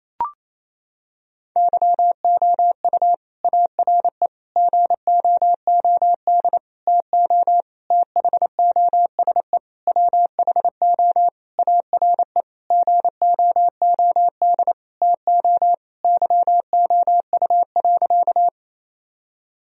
Morse code, heard as a keyed tone, about 27 wpm, 700 Hz. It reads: YOU ARE GOOD TO THOSE WHO ARE GOOD TO YOU.
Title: Sentences from Top 1000 Words - Review Part 1 - 28wpm